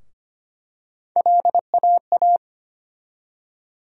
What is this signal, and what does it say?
Morse code, heard as a keyed tone, about 25 wpm, 700 Hz. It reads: LAA
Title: USA Call Sign Postfixes - 3 Letters 25wpm